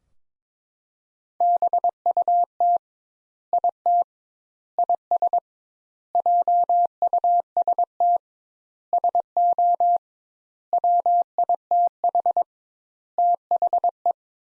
Morse code, heard as a keyed tone, about 22 wpm, 700 Hz. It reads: BUT IT IS JUST SO WITH THE
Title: Sentences Top 200 Words - 22wpm